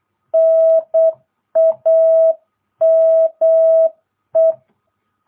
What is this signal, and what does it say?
Morse code, heard as a keyed tone, about 8 wpm, 650 Hz. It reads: NAME